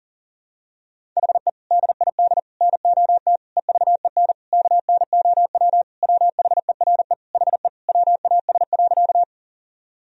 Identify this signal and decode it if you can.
Morse code, heard as a keyed tone, about 40 wpm, 700 Hz. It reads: HE DID NOT EVEN KNOW WHERE HE WAS.
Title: Sentences from Top 1000 Words - Review Part 1 - 40wpm